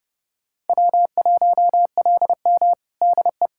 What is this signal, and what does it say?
Morse code, heard as a keyed tone, about 32 wpm, 700 Hz. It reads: W1LM DE
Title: USA Callsign DE Callsign - 4 Characters - 30wpm